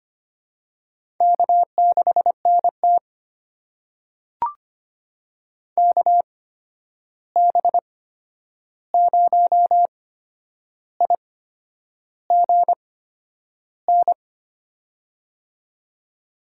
Morse code, heard as a keyed tone, about 25 wpm, spaced as for 9 wpm, 700 Hz. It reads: K6NT KB0IGN
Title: USA Call Signs - 8x Character Spacing - 25wpm